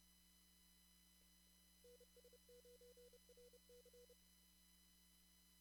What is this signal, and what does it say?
Morse code, heard as a keyed tone, about 30 wpm, 490 Hz. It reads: NS9RC
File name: CW / kt5j wpx cw-op-k5pi